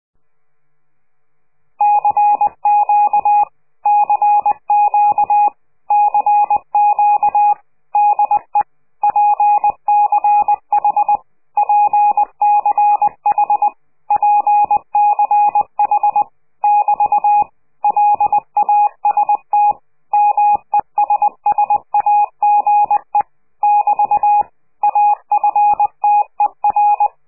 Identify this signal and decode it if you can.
Morse code, heard as a keyed tone, about 20 wpm, 830 Hz. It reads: CQ CQ CQ DE PCH PCH PCH <BT> LAST MESSAGE <BT> AFTER
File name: CQ de PCH laatste bericht farewell